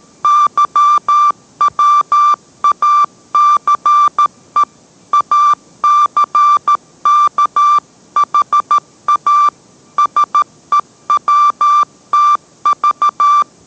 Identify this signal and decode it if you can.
Morse code, heard as a keyed tone, about 13 wpm, 1180 Hz. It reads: YWACE ACKHA SEWTV